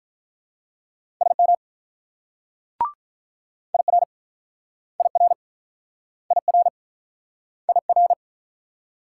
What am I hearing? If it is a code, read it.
Morse code, heard as a keyed tone, about 45 wpm, 700 Hz. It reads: SM IR IR IR IR